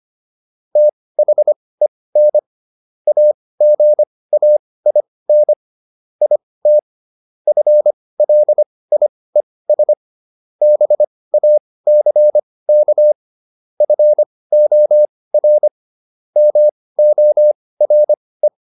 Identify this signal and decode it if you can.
Morse code, heard as a keyed tone, about 25 wpm, spaced as for 18 wpm, 600 Hz. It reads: THEN AGAIN IT FLIES BACK FOR MORE